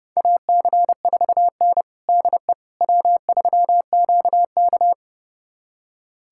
Morse code, heard as a keyed tone, about 29 wpm, 700 Hz. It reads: AC4N DE W3QK
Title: USA Callsign DE Callsign - 4 Characters - 30wpm